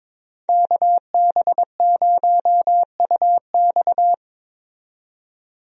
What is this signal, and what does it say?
Morse code, heard as a keyed tone, about 21 wpm, 700 Hz. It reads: KB0UX
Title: USA Call Signs - 5 Characters - 22wpm